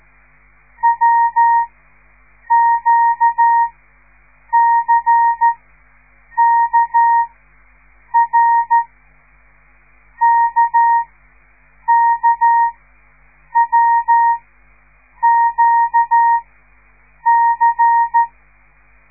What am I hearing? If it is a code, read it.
Morse code, heard as a keyed tone, about 13 wpm, spaced as for 7 wpm, 930 Hz. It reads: WQCKR KKWQC
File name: radiomorningstar / morse / les 5